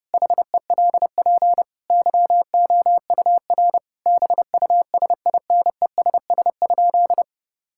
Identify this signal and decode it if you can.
Morse code, heard as a keyed tone, about 32 wpm, 700 Hz. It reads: HELP YOUR BUSINESS?
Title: Sentences from Top 1000 Words - Review Part 5 - 30wpm